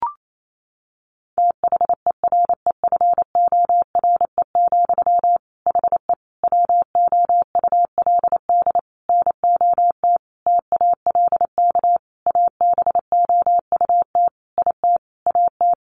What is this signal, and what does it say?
Morse code, heard as a keyed tone, about 28 wpm, 700 Hz. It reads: THEREFORE, HE WOULD NOT TALK ABOUT IT AT
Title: Sentences from Top 800 Words 28wpm